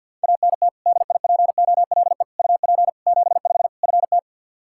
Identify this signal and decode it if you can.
Morse code, heard as a keyed tone, about 50 wpm, 700 Hz. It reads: ANT DIPOLE UP 75 FT
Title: K1USN SST QSO Practice 50wpm